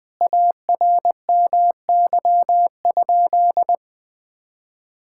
Morse code, heard as a keyed tone, about 20 wpm, 700 Hz. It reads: ARMY?
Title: Sentences from Top 1000 Words - Review Part 6 - 20wpm